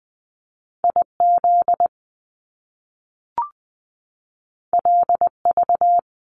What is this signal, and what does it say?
Morse code, heard as a keyed tone, about 20 wpm, 700 Hz. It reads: IZ LV
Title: Top Two Letter Combinations in Words 20wpm